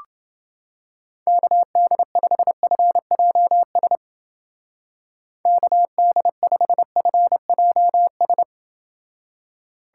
Morse code, heard as a keyed tone, about 30 wpm, 700 Hz. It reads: KD5FJS KD5FJS